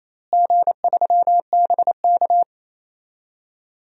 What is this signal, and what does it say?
Morse code, heard as a keyed tone, about 28 wpm, 700 Hz. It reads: G3BK
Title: DXCC-100 Call Signs - 4 Characters - Repeated 3x - 28wpm